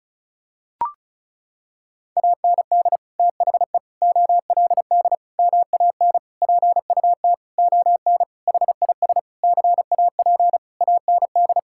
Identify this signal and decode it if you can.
Morse code, heard as a keyed tone, about 36 wpm, 700 Hz. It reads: AND THE OLD MAN PUT ON HIS CAP AND
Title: Sentences from Top 1600 Words 35wpm